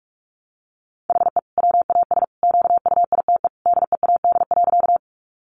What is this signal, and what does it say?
Morse code, heard as a keyed tone, about 44 wpm, 700 Hz. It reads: HE WAS QUITE DEAD.
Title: Sentences from Top 1500 Words 45wpm